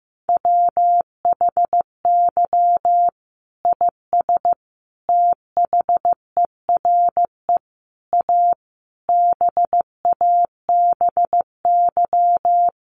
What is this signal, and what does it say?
Morse code, heard as a keyed tone, about 15 wpm, 700 Hz. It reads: WHY IS THERE A BABY
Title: Sentences from Top 1300 Words 15wpm